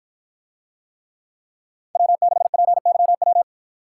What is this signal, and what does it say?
Morse code, heard as a keyed tone, about 48 wpm, 700 Hz. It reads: W6PYW